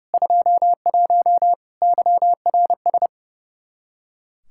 Morse code, heard as a keyed tone, about 30 wpm, 700 Hz. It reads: 21 YRS
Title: K1USN SST QSO Practice 30wpm